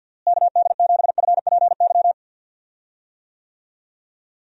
Morse code, heard as a keyed tone, about 47 wpm, 700 Hz. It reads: KD7FPY
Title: USA Call Signs - 6 Characters - Repeated 3x - 50wpm